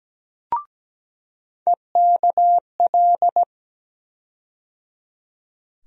Morse code, heard as a keyed tone, about 17 wpm, 700 Hz. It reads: EKL
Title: USA Call Sign Postfixes - 3 Letters 17wpm